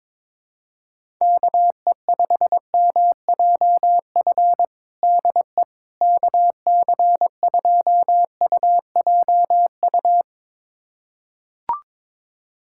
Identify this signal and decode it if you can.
Morse code, heard as a keyed tone, about 22 wpm, 700 Hz. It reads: KE5MJF DE KC2UJU
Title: USA Callsign DE Callsign - 6 Characters - 22wpm